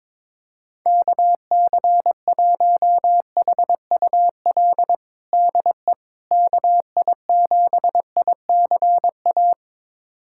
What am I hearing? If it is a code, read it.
Morse code, heard as a keyed tone, about 22 wpm, 700 Hz. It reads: KC1HUL DE KI7ICA
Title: USA Callsign DE Callsign - 6 Characters - 22wpm